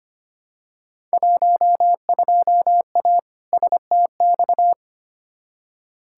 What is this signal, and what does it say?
Morse code, heard as a keyed tone, about 25 wpm, 700 Hz. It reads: 12A STX
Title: Field Day Exchanges Encore - 25wpm